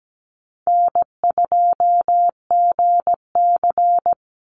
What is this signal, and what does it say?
Morse code, heard as a keyed tone, about 17 wpm, 700 Hz. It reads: N2GC
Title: Easy Call Signs - 17wpm